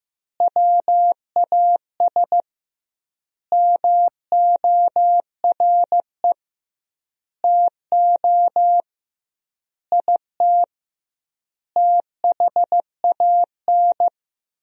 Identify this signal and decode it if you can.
Morse code, heard as a keyed tone, about 15 wpm, 700 Hz. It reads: WAS MORE TO IT THAN
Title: 2x Word-Spacing - Sentences from Top 100 Words - 15wpm